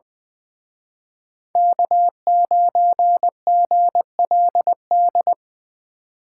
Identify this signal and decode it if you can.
Morse code, heard as a keyed tone, about 23 wpm, 700 Hz. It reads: K9GLD
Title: Speed Racing - USA 1x3 Call Signs 20wpm